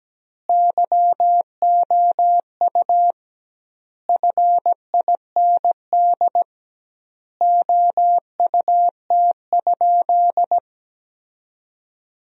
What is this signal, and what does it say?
Morse code, heard as a keyed tone, about 17 wpm, 700 Hz. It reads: YOU FIND OUT?
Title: 2x Word-Spacing - Sentences from Top 200 Words - 17wpm